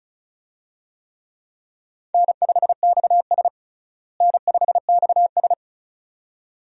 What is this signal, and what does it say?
Morse code, heard as a keyed tone, about 34 wpm, 700 Hz. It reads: N5XS N5XS
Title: USA Call Signs - 4 Characters - Repeated 2x - 35wpm